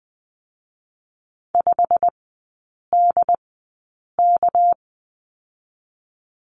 Morse code, heard as a keyed tone, about 20 wpm, 700 Hz. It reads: 5 D K
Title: 198 Lesson 32e - Letter k - Three Characters 20wpm